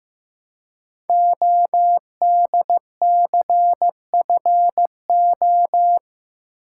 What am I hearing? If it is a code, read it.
Morse code, heard as a keyed tone, about 15 wpm, 700 Hz. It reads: ODCFO
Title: Code Groups - 5 Letters - 15wpm